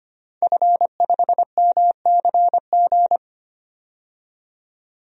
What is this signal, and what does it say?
Morse code, heard as a keyed tone, about 25 wpm, 700 Hz. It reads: F5MCG
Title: DXCC-100 Call Signs - 5 Characters - 25wpm